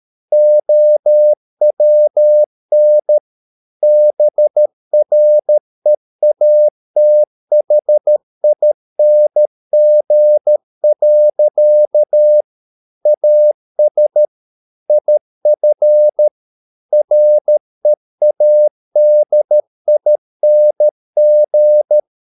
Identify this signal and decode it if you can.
Morse code, heard as a keyed tone, about 13 wpm, 600 Hz. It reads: OWN BREATHING. AS IF READING